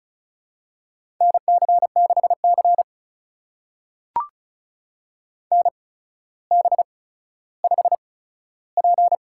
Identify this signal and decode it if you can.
Morse code, heard as a keyed tone, about 35 wpm, spaced as for 15 wpm, 700 Hz. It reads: NC6C NB5P
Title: USA Call Signs - 4 Characters - 8x Character Spacing - 35wpm